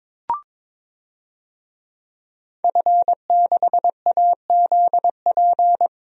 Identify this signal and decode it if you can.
Morse code, heard as a keyed tone, about 22 wpm, 700 Hz. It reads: F6AZP
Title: DXCC-100 Call Signs - 5 Characters - Repeated 2x - 22wpm